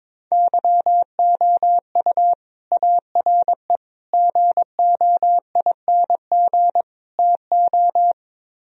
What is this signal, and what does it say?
Morse code, heard as a keyed tone, about 22 wpm, 700 Hz. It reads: YOU ARE GOING TO
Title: Speed-Racing - Sets of 4 Words 22wpm